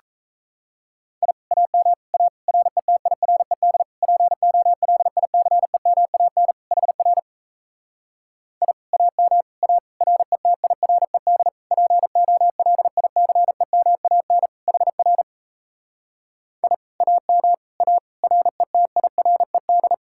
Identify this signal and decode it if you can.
Morse code, heard as a keyed tone, about 37 wpm, 700 Hz. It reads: I AM A RETIRED POLICEMAN HR I AM A RETIRED POLICEMAN HR I AM A RETIRED